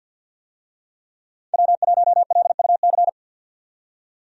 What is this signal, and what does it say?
Morse code, heard as a keyed tone, about 49 wpm, 700 Hz. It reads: W1LUC
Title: USA Call Signs - 5 Characters - 50wpm